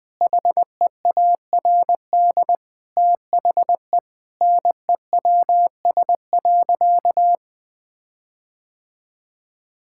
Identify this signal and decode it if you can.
Morse code, heard as a keyed tone, about 21 wpm, 700 Hz. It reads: HEARD THE NEWS.
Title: Sentences with 5-Word Phrases - 20wpm